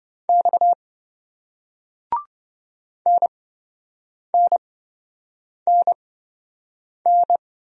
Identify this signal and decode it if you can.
Morse code, heard as a keyed tone, about 27 wpm, spaced as for 7 wpm, 700 Hz. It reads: X NNNN